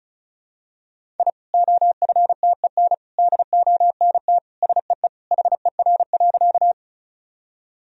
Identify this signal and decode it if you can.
Morse code, heard as a keyed tone, about 36 wpm, 700 Hz. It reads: I OFTEN DONT SEE HER.
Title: Sentences from Top 1000 Words - Review Part 1 - 35wpm